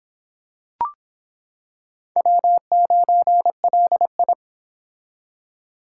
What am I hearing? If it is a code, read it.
Morse code, heard as a keyed tone, about 26 wpm, 700 Hz. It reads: W9LI